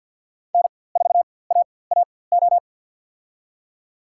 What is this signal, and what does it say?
Morse code, heard as a keyed tone, about 49 wpm, spaced as for 23 wpm, 700 Hz. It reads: N4AAO
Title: USA Call Signs - 5 Characters - 4x Character Spacing - 50wpm